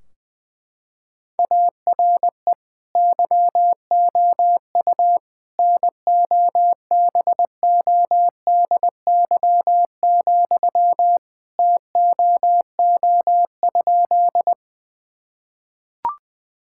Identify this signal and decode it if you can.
Morse code, heard as a keyed tone, about 21 wpm, 700 Hz. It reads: ARE YOU NOBODY, TOO?
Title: Sentences from Top 1600 Words 20wpm